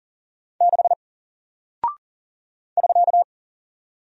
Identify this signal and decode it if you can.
Morse code, heard as a keyed tone, about 40 wpm, 700 Hz. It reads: 6 <SK>
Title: Single Character (Letter:Number:Punctuation:Prosign) Alt 40wpm